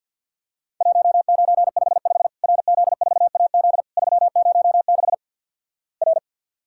Morse code, heard as a keyed tone, about 51 wpm, 700 Hz. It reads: 1955 RZ4AZ 306 R
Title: Worked All Europe DX Contest - QTC Exchange - 50wpm